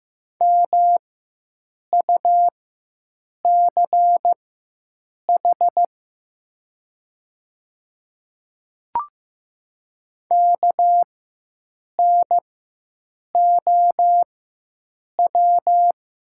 Morse code, heard as a keyed tone, about 15 wpm, spaced as for 7 wpm, 700 Hz. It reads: MUCH KNOW
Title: 4x Character Spacing - Top 200 Words - 15wpm